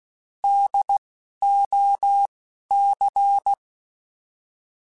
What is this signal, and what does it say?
Morse code, heard as a keyed tone, about 16 wpm, 800 Hz. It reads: D O C
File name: Alphabetical / Alphabetical mp3s / Ds @10